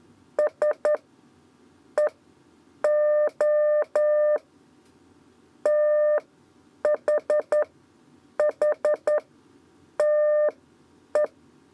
Morse code, heard as a keyed tone, about 11 wpm, spaced as for 8 wpm, 600 Hz. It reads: S E O TH H TE
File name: Lezione trasmessa 2 lez 2 exe